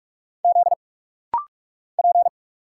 Morse code, heard as a keyed tone, about 46 wpm, 700 Hz. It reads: Z P
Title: Single Letter-Number Alt - Rapid-Fire 45wpm